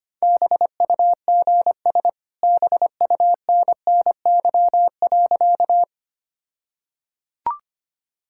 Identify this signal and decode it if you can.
Morse code, heard as a keyed tone, about 25 wpm, 700 Hz. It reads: BUGS BUNNY.
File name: practice / 25wpm / All-Age Appropriate Jokes - 25wpm